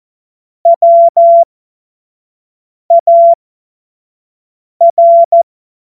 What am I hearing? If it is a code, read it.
Morse code, heard as a keyed tone, about 14 wpm, spaced as for 5 wpm, 680 Hz. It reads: WAR
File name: Morsekurs / Witze / Witz 03. Mai 2022 25 F70